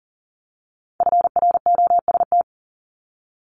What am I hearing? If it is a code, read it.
Morse code, heard as a keyed tone, about 41 wpm, 700 Hz. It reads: FROST
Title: Multi-letter Phonemes 40wpm